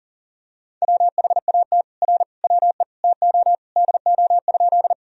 Morse code, heard as a keyed tone, about 40 wpm, 700 Hz. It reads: WHAT R WE TO DO?
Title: Sentences from Top 100 Words and Abbreviations 40wpm